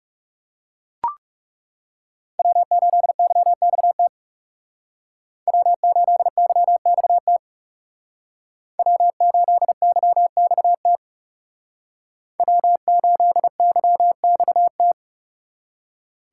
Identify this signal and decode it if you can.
Morse code, heard as a keyed tone, about 39 wpm, 700 Hz. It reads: W8YXT W8YXT W8YXT W8YXT